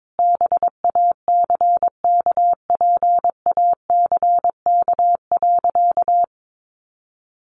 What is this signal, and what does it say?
Morse code, heard as a keyed tone, about 22 wpm, 700 Hz. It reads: BACKPACK.
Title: Additional Easy Phrases 22wpm